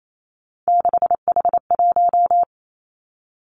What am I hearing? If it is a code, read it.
Morse code, heard as a keyed tone, about 28 wpm, 700 Hz. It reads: 6H1
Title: Three Challenging Letter-Number Combinations 28wpm